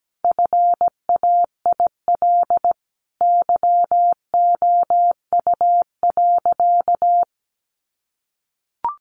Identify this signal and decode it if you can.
Morse code, heard as a keyed tone, about 17 wpm, 700 Hz. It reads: FAIL YOU.
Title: Sentences from Top 1100 Words 17wpm